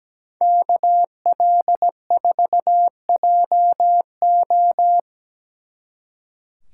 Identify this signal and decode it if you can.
Morse code, heard as a keyed tone, about 20 wpm, 700 Hz. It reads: KL4JO
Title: Speed Racing - USA 2x2 Call Signs 17wpm